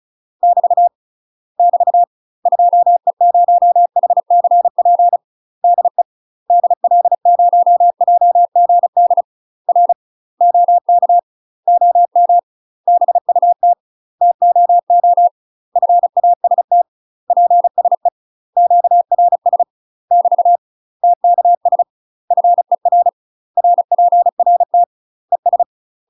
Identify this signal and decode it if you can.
Morse code, heard as a keyed tone, about 34 wpm, 700 Hz. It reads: <BT> <BT> 2E0HCP DE DL0JGD R OK OM BUT TOO FAST PSE QRS <BT> TKS FER RPRT ES
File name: UK MASTER QSOs TRAINING / 35WPM / UK MASTER QSOs 0008